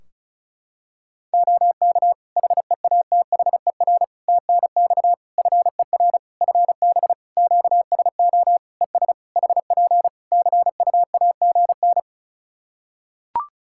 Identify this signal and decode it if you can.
Morse code, heard as a keyed tone, about 34 wpm, 700 Hz. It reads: OK HEATHER TNX FER FB QSO ES HP CUAGN
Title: K1USN SST QSO Practice 35wpm